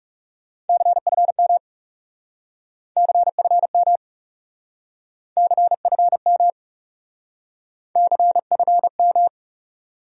Morse code, heard as a keyed tone, about 39 wpm, 700 Hz. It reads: CFM CFM CFM CFM